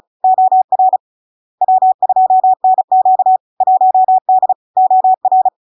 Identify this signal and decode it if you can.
Morse code, heard as a keyed tone, about 36 wpm, 750 Hz. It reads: OR W2NQ 1D OR
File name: LARK-FD-2022-Log-in-CW-35wpm-750-Hz